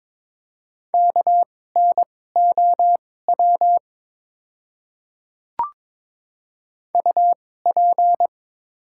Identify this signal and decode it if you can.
Morse code, heard as a keyed tone, about 22 wpm, spaced as for 16 wpm, 700 Hz. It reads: KNOW UP
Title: 2x Character Spacing - Top 200 Words - 22wpm